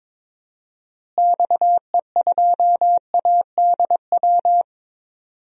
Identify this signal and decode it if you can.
Morse code, heard as a keyed tone, about 22 wpm, 700 Hz. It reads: XE2ADW
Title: DXCC-100 Call Signs - 6 Characters - Repeated 3x - 22wpm